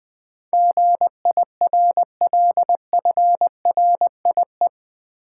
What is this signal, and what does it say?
Morse code, heard as a keyed tone, about 20 wpm, 700 Hz. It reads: GIRLFRIE
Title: Build A Word - 4 Letters At A Time - 10-Letter Words - 20wpm